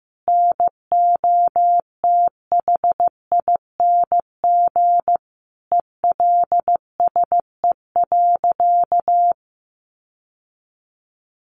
Morse code, heard as a keyed tone, about 15 wpm, 700 Hz. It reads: NOTHING ELSE.
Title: Sentences from Top 800 Words 15wpm